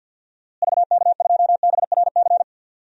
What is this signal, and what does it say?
Morse code, heard as a keyed tone, about 51 wpm, 700 Hz. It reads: VK2BRC
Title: DXCC-100 Call Signs - 6 Characters - Repeated 3x - 50wpm